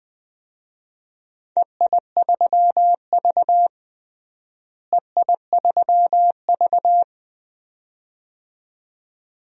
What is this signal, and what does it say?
Morse code, heard as a keyed tone, about 20 wpm, 700 Hz. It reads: EI3V EI3V
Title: DXCC-100 Call Signs - 4 Characters - Repeated 2x - 20wpm